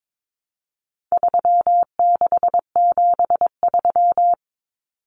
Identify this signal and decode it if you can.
Morse code, heard as a keyed tone, about 22 wpm, 700 Hz. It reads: 3673
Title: Common Cut Numbers - 4-Digit 22wpm